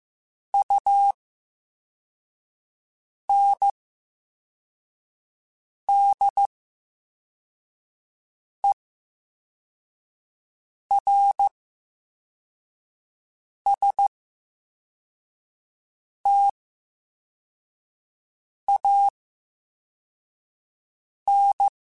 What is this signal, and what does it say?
Morse code, heard as a keyed tone, about 15 wpm, spaced as for 4 wpm, 800 Hz. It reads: UNDERSTAN
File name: Special INGs / Special INGs-mp3s / Slow ing @13